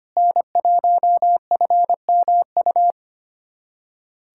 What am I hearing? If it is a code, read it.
Morse code, heard as a keyed tone, about 25 wpm, 700 Hz. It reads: N1FMU
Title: USA Callsign DE Callsign - 5 Characters - 25wpm